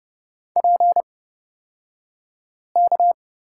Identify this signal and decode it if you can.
Morse code, heard as a keyed tone, about 30 wpm, spaced as for 5 wpm, 700 Hz. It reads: PK
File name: Lesson 37d - Letter x - Two Characters 30wpm